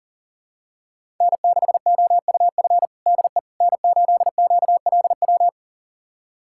Morse code, heard as a keyed tone, about 37 wpm, 700 Hz. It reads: N6OUF DE N8QLW
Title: USA Callsign DE Callsign - 5 Characters - 40wpm